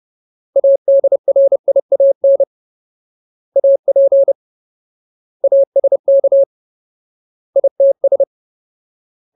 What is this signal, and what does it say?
Morse code, heard as a keyed tone, about 30 wpm, 550 Hz. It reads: ADRIAN AP ASK ITS
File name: Ultimate Word List / 30WPM Ultimate